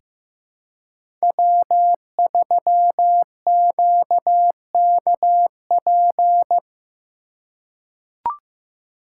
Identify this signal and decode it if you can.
Morse code, heard as a keyed tone, about 15 wpm, 700 Hz. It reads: W3QKP
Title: USA Call Signs - 5 Characters - 2x Character Spacing - 15wpm